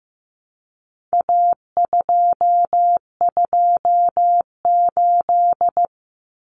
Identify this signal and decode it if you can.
Morse code, heard as a keyed tone, about 15 wpm, 700 Hz. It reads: A228
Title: Common Cut Numbers - 4-Digit 15wpm